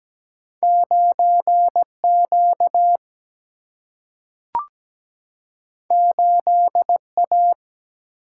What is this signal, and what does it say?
Morse code, heard as a keyed tone, about 17 wpm, 700 Hz. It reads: 9 Q 8 A
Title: Single-Letter Single-Number 17wpm